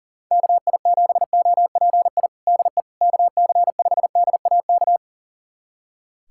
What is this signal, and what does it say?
Morse code, heard as a keyed tone, about 40 wpm, 700 Hz. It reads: KI7OPI DE KC5DAK
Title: USA Callsign DE Callsign 40wpm